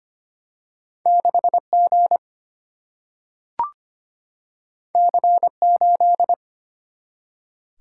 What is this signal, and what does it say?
Morse code, heard as a keyed tone, about 25 wpm, 700 Hz. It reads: 6G C8
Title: Single-Letter Single-Number 25wpm